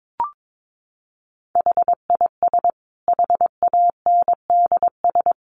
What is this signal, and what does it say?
Morse code, heard as a keyed tone, about 22 wpm, 700 Hz. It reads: HIS HANDS
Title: A Sound Of Thunder 22wpm